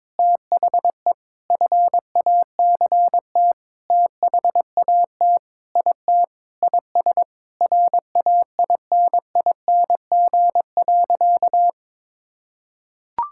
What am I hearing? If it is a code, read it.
Morse code, heard as a keyed tone, about 22 wpm, 700 Hz. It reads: THE FACT THAT IT IS RAINING.
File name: practice / 22wpm / Sentences with 8-Word Phrases - 22wpm